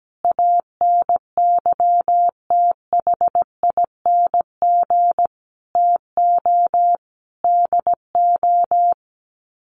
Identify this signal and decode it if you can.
Morse code, heard as a keyed tone, about 17 wpm, 700 Hz. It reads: ANYTHING TO DO
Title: Sets of 5 Words Encore 17wpm